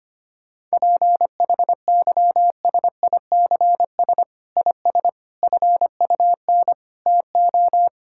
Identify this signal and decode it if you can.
Morse code, heard as a keyed tone, about 25 wpm, 700 Hz. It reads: PHYSICS IS FUN TO
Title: Sentences from Top 1000 Words - Review Part 5 - 25wpm